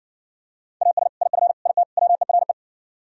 Morse code, heard as a keyed tone, about 60 wpm, 700 Hz. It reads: AS IF IT WERE